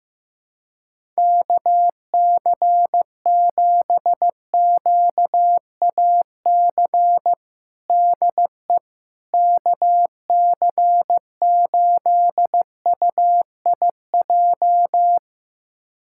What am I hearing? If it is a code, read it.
Morse code, heard as a keyed tone, about 15 wpm, 700 Hz. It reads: KC7QAC DE KC8UIJ
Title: USA Callsign DE Callsign - 6 Characters - 15wpm